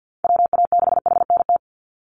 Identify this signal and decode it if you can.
Morse code, heard as a keyed tone, about 50 wpm, 700 Hz. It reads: WA6HNT
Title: USA Call Signs 50wpm